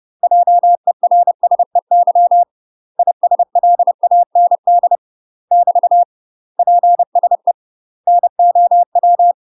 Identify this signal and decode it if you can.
Morse code, heard as a keyed tone, about 30 wpm, 700 Hz. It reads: JERSEY ISLAND <BT> PSE NOW
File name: UK MASTER QSOs TRAINING/30WPM